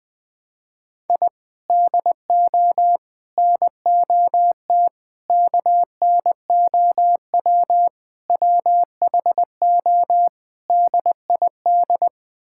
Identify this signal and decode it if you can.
Morse code, heard as a keyed tone, about 20 wpm, 700 Hz. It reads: I DO NOT KNOW WHO DID
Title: Sentences with 5-Word Phrases - 20wpm